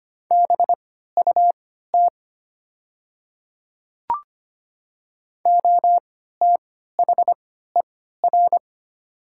Morse code, heard as a keyed tone, about 25 wpm, spaced as for 14 wpm, 700 Hz. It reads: BUT OTHER
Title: 3x Character Spacing - Top 100 Words - 25wpm